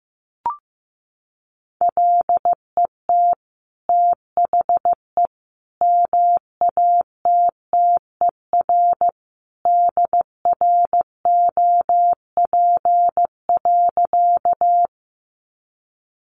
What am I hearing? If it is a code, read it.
Morse code, heard as a keyed tone, about 15 wpm, 700 Hz. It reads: LET THE MATTER DROP.
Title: Sentences from Top 1400 Words 15wpm